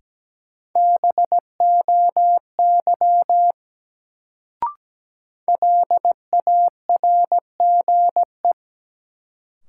Morse code, heard as a keyed tone, about 17 wpm, 700 Hz. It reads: BOY LARGE
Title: Sentences from Top 200 Words - New Word Drill - 17wpm